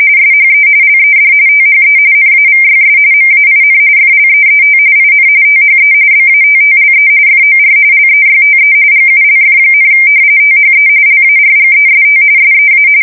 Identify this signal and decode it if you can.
RTTY, about 45.45 baud. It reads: SHE LED THE NAVYS MAIN FLEET IN THE BATTLE FOR LEYTE GULF, THE LARGEST SEA
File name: fsk170 / battleshipnj